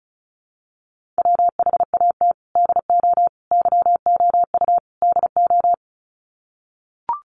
Morse code, heard as a keyed tone, about 34 wpm, 700 Hz. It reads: WHAT DO YOU DO
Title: Sets of 4 Words 35wpm